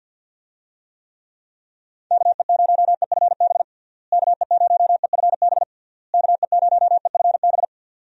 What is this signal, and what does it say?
Morse code, heard as a keyed tone, about 50 wpm, 700 Hz. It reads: KE0EFB KE0EFB KE0EFB
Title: USA Call Signs - 6 Characters - Repeated 3x - 50wpm